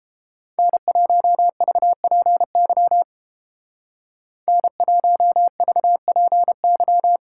Morse code, heard as a keyed tone, about 29 wpm, 700 Hz. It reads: N1VPY N1VPY